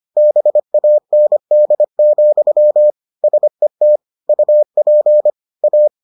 Morse code, heard as a keyed tone, about 25 wpm, 600 Hz. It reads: BAND, SET UP A